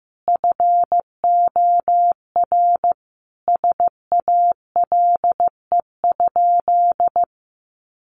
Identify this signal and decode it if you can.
Morse code, heard as a keyed tone, about 15 wpm, 700 Hz. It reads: FOR SALE?
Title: Sentences from Top 1400 Words 15wpm